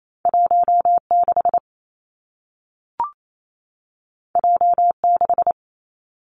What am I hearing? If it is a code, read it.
Morse code, heard as a keyed tone, about 28 wpm, 700 Hz. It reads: 1 6 J 6
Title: Two Challenging Letter-Number Combinations 28wpm